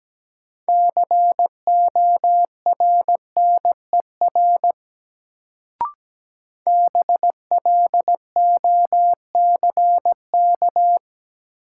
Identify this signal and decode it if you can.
Morse code, heard as a keyed tone, about 17 wpm, 700 Hz. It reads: CORNER BLOCK
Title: Sentences from Top 1600 Words - New Word Drill - 17wpm